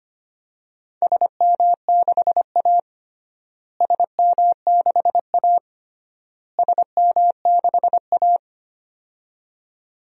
Morse code, heard as a keyed tone, about 25 wpm, 700 Hz. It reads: SM6A SM6A SM6A
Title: DXCC-100 Call Signs - 4 Characters - Repeated 3x - 25wpm